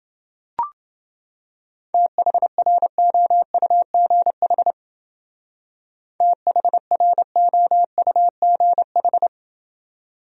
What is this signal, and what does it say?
Morse code, heard as a keyed tone, about 29 wpm, 700 Hz. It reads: THROUGH THROUGH